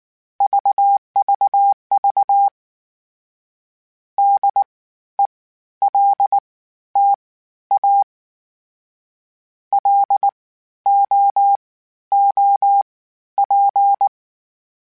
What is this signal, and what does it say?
Morse code, heard as a keyed tone, about 19 wpm, spaced as for 12 wpm, 800 Hz. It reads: VVV DELTA LOOP
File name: (spaced) / FromSMs spaced mp3s / Antennas sp@18